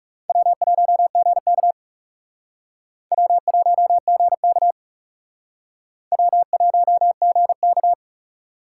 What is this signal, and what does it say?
Morse code, heard as a keyed tone, about 35 wpm, 700 Hz. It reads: W1GK W1GK W1GK
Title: Speed Racing - USA 1x2 Call Signs 30wpm